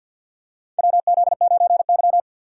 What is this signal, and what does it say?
Morse code, heard as a keyed tone, about 51 wpm, 700 Hz. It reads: WZ9Y
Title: USA Call Signs - 4 Characters - 2x Character Spacing - 50wpm